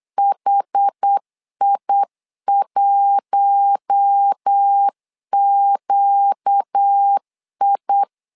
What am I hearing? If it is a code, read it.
Morse code, heard as a keyed tone, about 8 wpm, 800 Hz. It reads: HI1QI
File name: LEZIONI / 4 Q1 8wpm 3035